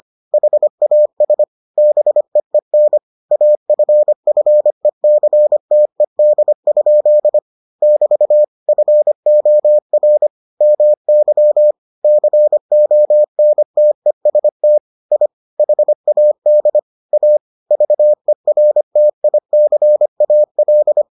Morse code, heard as a keyed tone, about 25 wpm, 600 Hz. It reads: HAS BEEN AFFECTED? <BT> FOR MY CONTEST I HAD A VERTICAL